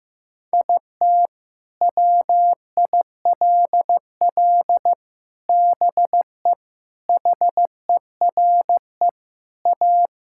Morse code, heard as a keyed tone, about 15 wpm, 700 Hz. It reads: IT WILL BE HERE A
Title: Sentences from Top 1000 Words - Review Part 5 - 15wpm